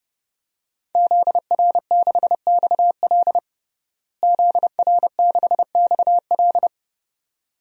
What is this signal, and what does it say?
Morse code, heard as a keyed tone, about 30 wpm, 700 Hz. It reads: ZR6XL ZR6XL
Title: DXCC-100 Call Signs - 5 Characters - Repeated 2x - 30wpm